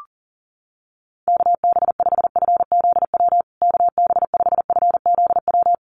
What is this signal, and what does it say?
Morse code, heard as a keyed tone, about 40 wpm, 700 Hz. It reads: KB5FZW KB5FZW
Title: USA Call Signs Repeated 3x 40wpm